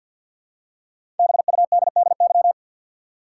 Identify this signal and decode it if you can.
Morse code, heard as a keyed tone, about 51 wpm, 700 Hz. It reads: BUDDY
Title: Five Letter Words - 50wpm